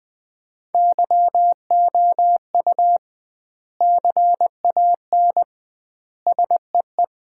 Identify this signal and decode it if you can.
Morse code, heard as a keyed tone, about 20 wpm, 700 Hz. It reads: YOU CAN SEE
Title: 2x Word-Spacing - Sets of 3 Words - 20wpm